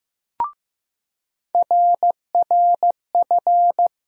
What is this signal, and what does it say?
Morse code, heard as a keyed tone, about 15 wpm, 700 Hz. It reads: RRF
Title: USA Call Sign Postfixes - 3 Letters 15wpm